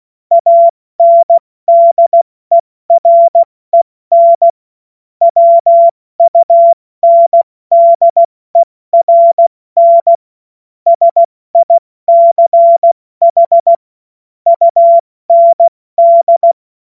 Morse code, heard as a keyed tone, about 16 wpm, 680 Hz. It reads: ANDEREN WUNDERN SICH UND